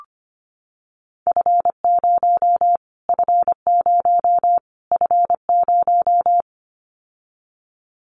Morse code, heard as a keyed tone, about 25 wpm, 700 Hz. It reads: F0 F0 F0
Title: Callsign Prefixes for Top 25 DXCC - Repeated 3x - 25wpm